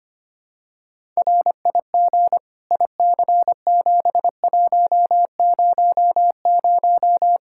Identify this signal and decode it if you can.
Morse code, heard as a keyed tone, about 25 wpm, 700 Hz. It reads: RIG IC7100
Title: Practice QSO - Exchange 2 - Info - 25wpm